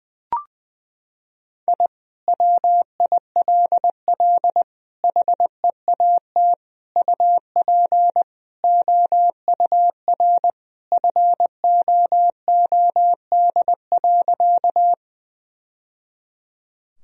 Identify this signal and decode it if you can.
Morse code, heard as a keyed tone, about 21 wpm, 700 Hz. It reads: I WILL HEAT UP OUR FOOD.
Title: Sentences from Top 1000 Words - Review Part 3 - 20wpm